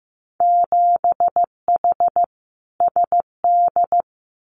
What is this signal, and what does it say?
Morse code, heard as a keyed tone, about 15 wpm, 700 Hz. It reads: 7H SD
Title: Winter Field Day Exchanges Encore 15wpm